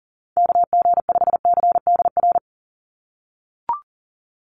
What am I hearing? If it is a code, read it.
Morse code, heard as a keyed tone, about 40 wpm, 700 Hz. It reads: KG5CDR
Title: USA Call Signs 40wpm